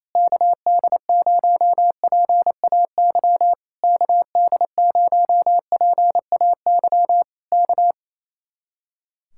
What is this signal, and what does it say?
Morse code, heard as a keyed tone, about 27 wpm, 700 Hz. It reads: KD0PAY KD0PAY K
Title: Speed-Racing Common QSO Phrases Encore - 28wpm